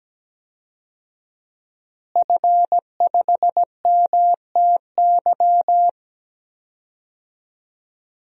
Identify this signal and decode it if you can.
Morse code, heard as a keyed tone, about 17 wpm, 700 Hz. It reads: F5MTY
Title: DXCC-100 Call Signs - 5 Characters - Repeated 3x - 17wpm